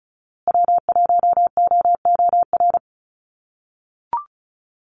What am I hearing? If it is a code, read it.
Morse code, heard as a keyed tone, about 34 wpm, 700 Hz. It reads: W1OOR
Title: USA Call Signs Repeated 2x 35wpm